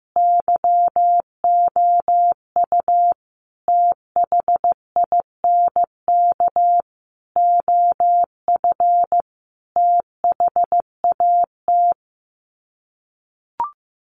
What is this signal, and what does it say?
Morse code, heard as a keyed tone, about 15 wpm, 700 Hz. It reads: YOU THINK OF THAT
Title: Sets of 5 Words Encore2 15wpm